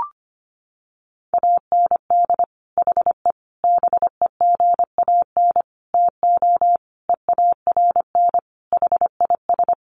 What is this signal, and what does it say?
Morse code, heard as a keyed tone, about 25 wpm, 700 Hz. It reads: AND HE BEGAN TO EARN HIS
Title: Sentences from Top 1500 Words 25wpm